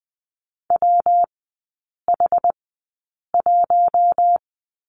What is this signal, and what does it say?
Morse code, heard as a keyed tone, about 20 wpm, spaced as for 8 wpm, 700 Hz. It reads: WH1
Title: 114 Lesson 19e - Letter w - Three Characters 20wpm